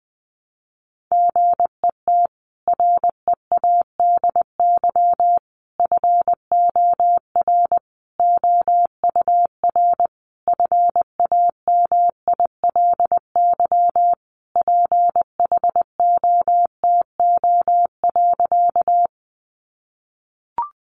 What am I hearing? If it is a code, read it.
Morse code, heard as a keyed tone, about 20 wpm, 700 Hz. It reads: GET READY FOR OUR FAMILY PHOTO.
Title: Sentences from Top 1200 Words 20wpm